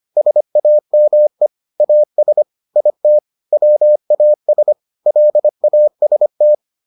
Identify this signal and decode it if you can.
Morse code, heard as a keyed tone, about 25 wpm, 600 Hz. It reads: SAME AS IT WAS LAST